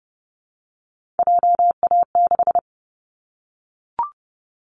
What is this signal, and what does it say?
Morse code, heard as a keyed tone, about 30 wpm, 700 Hz. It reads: JA6
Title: Callsign Prefixes for Top 25 DXCC - Repeated 3x - 30wpm